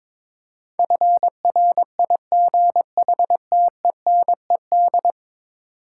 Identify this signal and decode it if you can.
Morse code, heard as a keyed tone, about 22 wpm, 700 Hz. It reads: FRIGHTENED
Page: morse-code-ninja-practice-22wpm